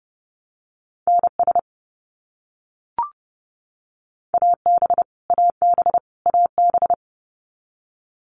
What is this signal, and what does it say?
Morse code, heard as a keyed tone, about 30 wpm, 700 Hz. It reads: NS AB AB AB
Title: Canadian Provinces and Territories - Repeated 3x - 30wpm